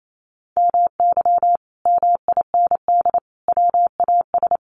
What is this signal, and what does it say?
Morse code, heard as a keyed tone, about 28 wpm, 700 Hz. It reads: MY MIND WAS
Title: Sentences from Top 500 Words 28wpm